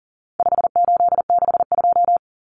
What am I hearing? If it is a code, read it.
Morse code, heard as a keyed tone, about 40 wpm, 700 Hz. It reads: 5862
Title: Common Cut Numbers - 4-Digit 40wpm